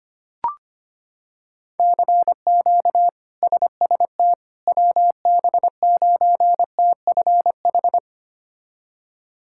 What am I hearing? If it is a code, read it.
Morse code, heard as a keyed tone, about 25 wpm, 700 Hz. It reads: CQ SST WB9TFH
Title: K1USN SST QSO Practice - 25wpm